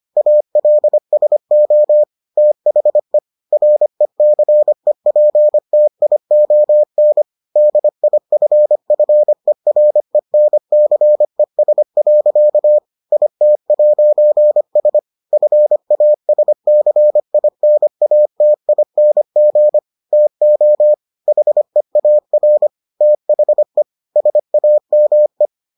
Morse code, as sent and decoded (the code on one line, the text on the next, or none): .- .-.. ... --- / - .... / .-. -.-. .--. - .. --- -. / -.. .. ..-. ..-. .-. -. -.-. ... .-.-.- / .. - .----. ... / ..-. .- ... -.-. .. -. .- - .. -. --. / - --- / .... .- .-. / - .... / ... .- -- .
ALSO THE RECEPTION DIFFERENCES. IT'S FASCINATING TO HEAR THE SAME